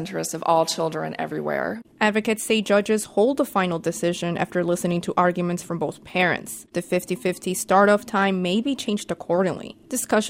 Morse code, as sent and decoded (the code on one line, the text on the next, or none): none